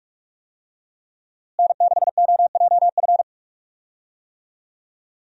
Morse code, heard as a keyed tone, about 46 wpm, 700 Hz. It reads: N6OJF
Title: USA Call Signs - 5 Characters - Repeated 2x - 45wpm